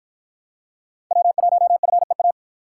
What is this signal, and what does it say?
Morse code, heard as a keyed tone, about 48 wpm, 700 Hz. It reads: W1FEA